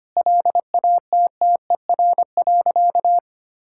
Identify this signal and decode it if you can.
Morse code, heard as a keyed tone, about 25 wpm, 700 Hz. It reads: LATTER.
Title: Sentences from Top 1600 Words 25wpm